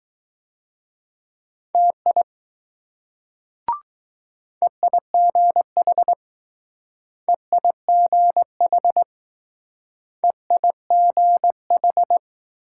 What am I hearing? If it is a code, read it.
Morse code, heard as a keyed tone, about 19 wpm, 700 Hz. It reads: TI EIGH EIGH EIGH